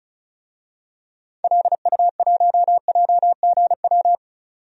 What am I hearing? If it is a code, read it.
Morse code, heard as a keyed tone, about 34 wpm, 700 Hz. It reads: LU1JGW
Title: DXCC-100 Call Signs - 6 Characters - 35wpm